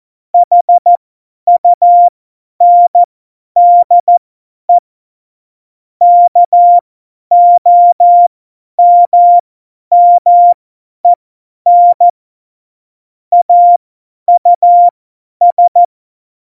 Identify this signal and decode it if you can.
Morse code, heard as a keyed tone, about 14 wpm, spaced as for 10 wpm, 700 Hz. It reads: HUNDE KOMMEN AUS
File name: Morsekurs / Witze / Witz 11. September 2018, 50 F70